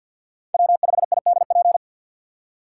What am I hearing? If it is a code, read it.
Morse code, heard as a keyed tone, about 61 wpm, 700 Hz. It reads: W5IDP
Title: Speed Racing - USA Call Signs 50wpm